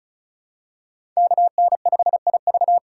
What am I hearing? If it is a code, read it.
Morse code, heard as a keyed tone, about 36 wpm, 700 Hz. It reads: KN5IV
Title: Code Groups - 5 Letter:Numbers - 35wpm